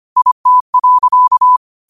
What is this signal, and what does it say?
Morse code, heard as a keyed tone, about 25 wpm, 1000 Hz. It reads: IT.
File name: qotdfiles / 25wpm